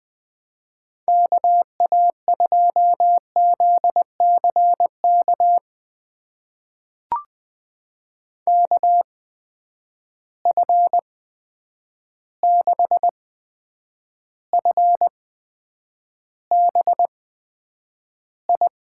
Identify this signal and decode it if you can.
Morse code, heard as a keyed tone, about 21 wpm, spaced as for 9 wpm, 700 Hz. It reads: KA2ZCK KF6FBI
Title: USA Call Signs - 6 Characters - 8x Character Spacing - 20wpm